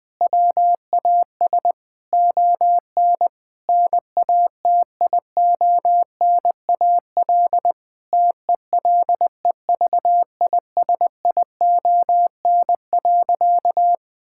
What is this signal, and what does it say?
Morse code, heard as a keyed tone, about 20 wpm, 700 Hz. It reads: WAS ON NATIONAL TELEVISION.
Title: Sentences from Top 1000 Words - Review Part 5 - 20wpm